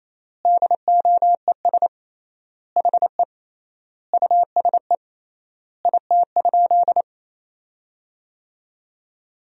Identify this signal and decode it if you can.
Morse code, heard as a keyed tone, about 28 wpm, 700 Hz. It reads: DOES HE USE IT?
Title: 3x Word-Spacing - Sentences from Top 200 Words - 28wpm